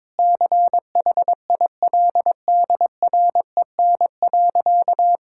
Morse code, heard as a keyed tone, about 22 wpm, 700 Hz. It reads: CHILDREN.
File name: Difficult Sentences with Hints - 22wpm